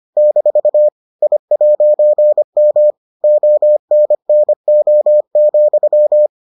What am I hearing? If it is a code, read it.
Morse code, heard as a keyed tone, about 25 wpm, 600 Hz. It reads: - I'M ONNO,